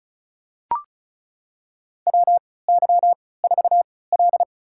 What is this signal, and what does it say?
Morse code, heard as a keyed tone, about 36 wpm, spaced as for 20 wpm, 700 Hz. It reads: WY4L
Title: USA Call Signs - 4 Characters - 3x Character Spacing - 35wpm